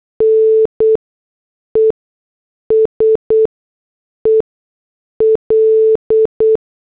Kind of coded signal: Morse code